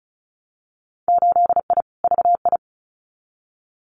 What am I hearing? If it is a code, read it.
Morse code, heard as a keyed tone, about 36 wpm, 700 Hz. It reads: 8I VI
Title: Winter Field Day Exchanges Encore 35wpm